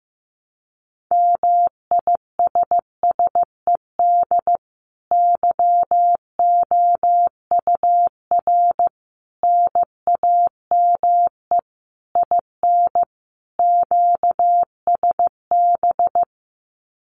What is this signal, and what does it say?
Morse code, heard as a keyed tone, about 15 wpm, 700 Hz. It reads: MISSED YOUR NAME IN QSB
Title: Common QSO Phrases 15wpm